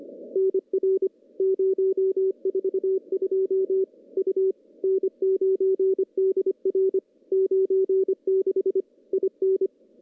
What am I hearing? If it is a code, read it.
Morse code, heard as a keyed tone, about 25 wpm, 410 Hz. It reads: NR 042 U N9DR 96 IN
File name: CW-SS-k5tr